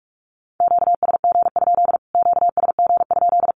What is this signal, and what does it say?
Morse code, heard as a keyed tone, about 44 wpm, 700 Hz. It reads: QSG? QSG?
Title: Q-Codes Repeated 3x 45wpm